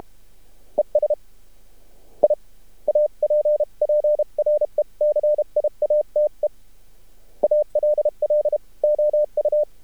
Morse code, heard as a keyed tone, about 32 wpm, 600 Hz. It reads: ES I APPRECIATE ALL OU
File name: archived / CRSnet